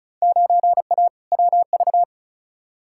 Morse code, heard as a keyed tone, about 36 wpm, 700 Hz. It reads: 9A WV
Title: Field Day Exchanges - 35wpm